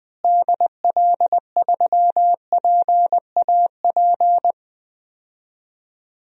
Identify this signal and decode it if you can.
Morse code, heard as a keyed tone, about 20 wpm, 700 Hz. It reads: DL3PAP
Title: DXCC-100 Call Signs - 6 Characters - Repeated 2x - 20wpm